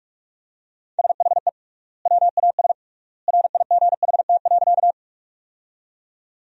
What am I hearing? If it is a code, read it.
Morse code, heard as a keyed tone, about 46 wpm, 700 Hz. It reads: SHE WAS RIGHT.
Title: 3x Word-Spacing - Sentences from Top 200 Words - 45wpm